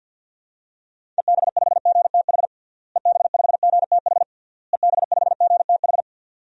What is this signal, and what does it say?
Morse code, heard as a keyed tone, about 46 wpm, 700 Hz. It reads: EB5GTH EB5GTH EB5GTH
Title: DXCC-100 Call Signs - 6 Characters - Repeated 3x - 50wpm